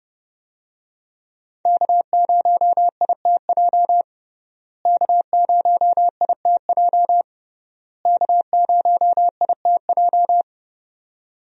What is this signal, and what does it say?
Morse code, heard as a keyed tone, about 30 wpm, 700 Hz. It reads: K0ITJ K0ITJ K0ITJ
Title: USA Call Signs - 5 Characters - Repeated 3x - 30wpm